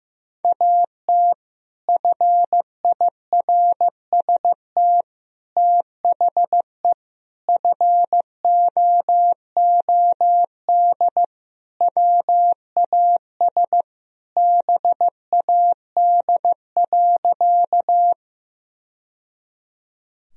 Morse code, heard as a keyed tone, about 15 wpm, 700 Hz. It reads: AT FIRST THE FOOD WAS BAD.
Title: Sentences from Top 1000 Words - Review Part 1 - 15wpm